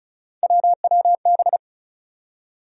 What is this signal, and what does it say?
Morse code, heard as a keyed tone, about 36 wpm, 700 Hz. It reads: WWB
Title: USA Call Sign Postfixes - 3 Letters 35wpm